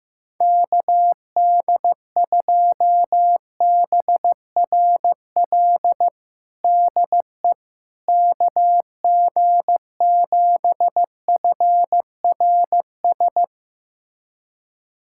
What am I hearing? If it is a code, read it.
Morse code, heard as a keyed tone, about 15 wpm, 700 Hz. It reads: KD2BRL DE KG7FRS
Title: USA Callsign DE Callsign 15wpm